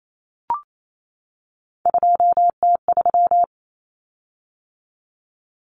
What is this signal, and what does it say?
Morse code, heard as a keyed tone, about 28 wpm, 700 Hz. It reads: 2T3
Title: Common Cut Numbers - 3-Digit 28wpm